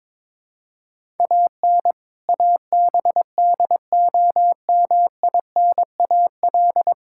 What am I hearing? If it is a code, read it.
Morse code, heard as a keyed tone, about 22 wpm, 700 Hz. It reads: AN ABDOMINAL
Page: morse-code-ninja-practice-22wpm